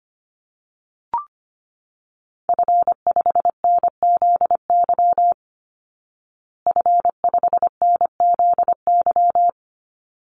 Morse code, heard as a keyed tone, about 25 wpm, 700 Hz. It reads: F5NZY F5NZY
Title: Call Signs Repeated 3x - 25wpm